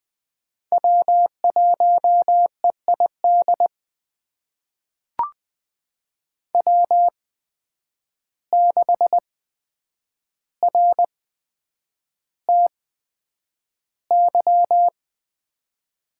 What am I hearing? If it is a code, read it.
Morse code, heard as a keyed tone, about 20 wpm, spaced as for 9 wpm, 700 Hz. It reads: W1EID W6RTY
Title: USA Call Signs - 5 Characters - 8x Character Spacing - 20wpm